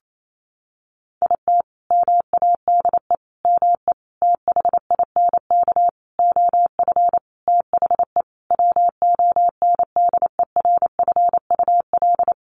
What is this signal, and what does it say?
Morse code, heard as a keyed tone, about 28 wpm, 700 Hz. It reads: IT MADE ME THINK OF THE WONDERFUL
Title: Sentences from Top 1100 Words 28wpm